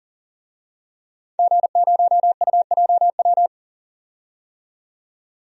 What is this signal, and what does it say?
Morse code, heard as a keyed tone, about 40 wpm, 700 Hz. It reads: G0UJW
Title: DXCC-100 Call Signs - 5 Characters - 40wpm